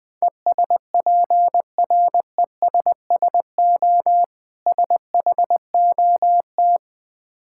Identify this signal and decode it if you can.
Morse code, heard as a keyed tone, about 20 wpm, 700 Hz. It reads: ESPRESSO SHOT
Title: Difficult Words with Hints - 20wpm